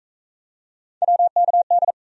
Morse code, heard as a keyed tone, about 40 wpm, 700 Hz. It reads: WKD